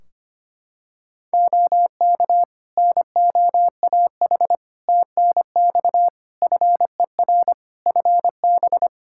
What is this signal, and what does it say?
Morse code, heard as a keyed tone, about 25 wpm, 700 Hz. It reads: OK NOAH TNX FER FB
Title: Practice QSO - Exchange 4 - Ending - 25wpm